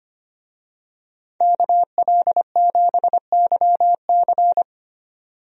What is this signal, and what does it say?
Morse code, heard as a keyed tone, about 25 wpm, 700 Hz. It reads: KL7YC
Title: DXCC-100 Call Signs - 5 Characters - 25wpm